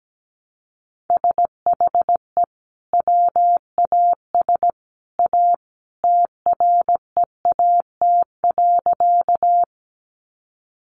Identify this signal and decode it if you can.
Morse code, heard as a keyed tone, about 17 wpm, 700 Hz. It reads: SHE WAS A TREAT.
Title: Sentences from Top 1300 Words 17wpm